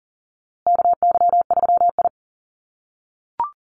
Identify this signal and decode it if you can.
Morse code, heard as a keyed tone, about 40 wpm, 700 Hz. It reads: KY3I
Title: USA Call Signs Repeated 2x 40wpm